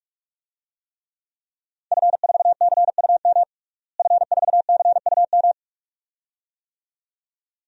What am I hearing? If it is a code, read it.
Morse code, heard as a keyed tone, about 46 wpm, 700 Hz. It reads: F4CUM F4CUM
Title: DXCC-100 Call Signs - 5 Characters - Repeated 2x - 45wpm